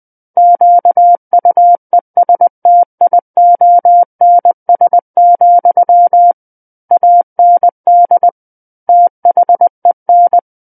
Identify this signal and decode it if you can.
Morse code, heard as a keyed tone, about 20 wpm, 700 Hz. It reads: QUESTIONS, AND THEN
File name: 20 WPM